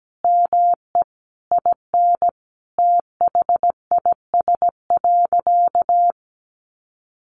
Morse code, heard as a keyed tone, about 17 wpm, 700 Hz. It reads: ME IN THIS.
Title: Sentences from Top 1200 Words 17wpm